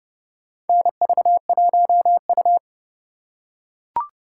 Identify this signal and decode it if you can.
Morse code, heard as a keyed tone, about 30 wpm, 700 Hz. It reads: NV1U
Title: K1USN SST QSO Practice - 30wpm